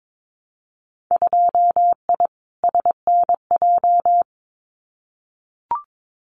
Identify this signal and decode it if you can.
Morse code, heard as a keyed tone, about 22 wpm, 700 Hz. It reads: 2I SNJ
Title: Winter Field Day Exchanges Encore 22wpm